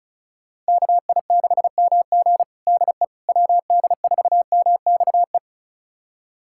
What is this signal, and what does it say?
Morse code, heard as a keyed tone, about 34 wpm, 700 Hz. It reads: KI6MG DE WD4MXE
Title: USA Callsign DE Callsign 35wpm